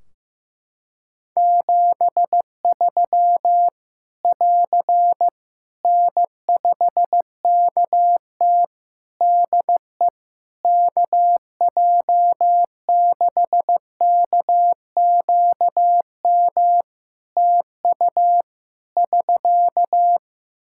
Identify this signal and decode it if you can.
Morse code, heard as a keyed tone, about 15 wpm, 700 Hz. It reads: 73 <AR> N5KT DE KJ6KQM TU <SK>
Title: Practice QSO - Exchange 4 - Ending - 15wpm